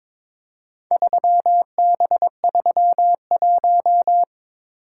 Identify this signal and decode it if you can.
Morse code, heard as a keyed tone, about 22 wpm, 700 Hz. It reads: 3B31
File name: practice / 22wpm / Four Letter-Number Combinations - 22wpm